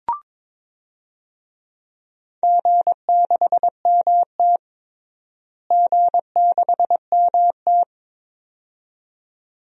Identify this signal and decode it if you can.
Morse code, heard as a keyed tone, about 22 wpm, 700 Hz. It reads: G6MT G6MT
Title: DXCC-100 Call Signs - 4 Characters - Repeated 2x - 22wpm